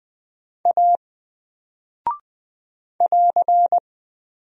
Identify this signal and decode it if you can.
Morse code, heard as a keyed tone, about 20 wpm, 700 Hz. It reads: A <AR>
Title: Single Character (Letter:Number:Punctuation:Prosign) 20wpm